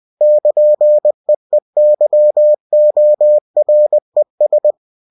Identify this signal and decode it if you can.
Morse code, heard as a keyed tone, about 20 wpm, 600 Hz. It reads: <KN>EEYORES
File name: cw-books / winnie-the-pooh / winnie-20wpm-page-0099